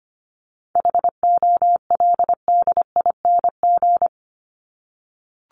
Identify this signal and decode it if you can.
Morse code, heard as a keyed tone, about 25 wpm, 700 Hz. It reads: HOLDING
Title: ING Suffix - Top 100 Words with Base Words 25wpm